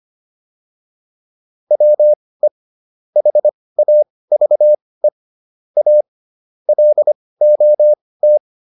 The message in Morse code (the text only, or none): WE HAVE A LOT